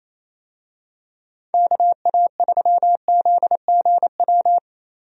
Morse code, heard as a keyed tone, about 28 wpm, 700 Hz. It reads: KA3ZGW
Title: USA Call Signs - 6 Characters - Repeated 3x - 28wpm